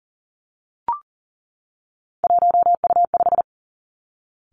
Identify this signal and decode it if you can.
Morse code, heard as a keyed tone, about 38 wpm, 700 Hz. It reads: 1U5
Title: Three Challenging Letter-Number Combinations 40wpm